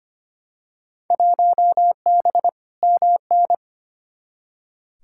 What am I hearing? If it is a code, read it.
Morse code, heard as a keyed tone, about 25 wpm, 700 Hz. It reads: 1B MN
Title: Field Day Exchanges Encore - 25wpm